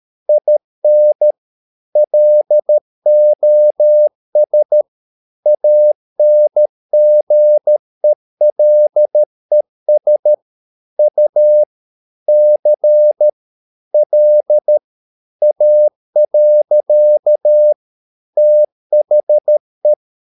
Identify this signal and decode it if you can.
Morse code, heard as a keyed tone, about 13 wpm, 600 Hz. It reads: IN LOS ANGELES U C L A. THE
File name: mp3 files / Lexiphiles - 13wpm